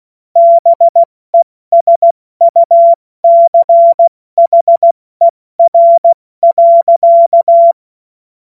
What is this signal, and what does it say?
Morse code, heard as a keyed tone, about 16 wpm, 680 Hz. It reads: BESUCHER.
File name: Morsekurs / Witze / Witz 10. Mai 2022 70 F80